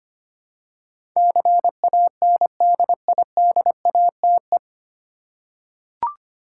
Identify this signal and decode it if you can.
Morse code, heard as a keyed tone, about 25 wpm, 700 Hz. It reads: CANDIDATE
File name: Nine Letter Words - 25wpm